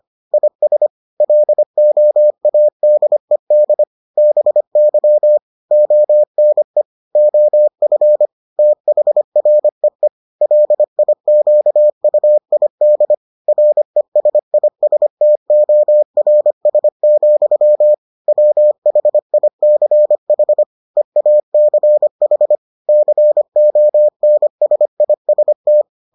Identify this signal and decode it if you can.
Morse code, heard as a keyed tone, about 25 wpm, 600 Hz. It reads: IS LOADED BY ONE OF THREE LIQUID RESISTORS, WHICH EACH CONSIST